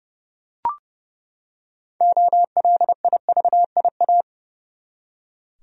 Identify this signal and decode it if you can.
Morse code, heard as a keyed tone, about 32 wpm, 700 Hz. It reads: OLIVIA
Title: USA Common Names - 30wpm